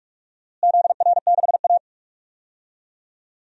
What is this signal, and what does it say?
Morse code, heard as a keyed tone, about 43 wpm, 700 Hz. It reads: ZR6A
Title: DXCC-100 Call Signs - 4 Characters - Repeated 2x - 45wpm